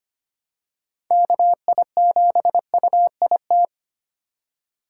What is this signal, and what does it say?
Morse code, heard as a keyed tone, about 25 wpm, 700 Hz. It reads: KI7UIT
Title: USA Call Signs - 6 Characters - 2x Character Spacing - 25wpm